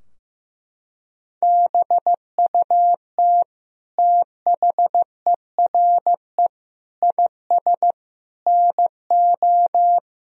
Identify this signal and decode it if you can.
Morse code, heard as a keyed tone, about 16 wpm, 700 Hz. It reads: BUT THERE IS NO
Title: Sentences from Top 1000 Words - Review Part 3 - 15wpm